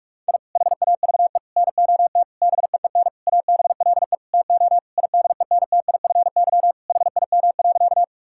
Morse code, heard as a keyed tone, about 41 wpm, 700 Hz. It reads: I HAVE NOT BEEN ABLE TO IDENTIFY HIM.
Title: Sentences from Top 1000 Words - Review Part 6 - 45wpm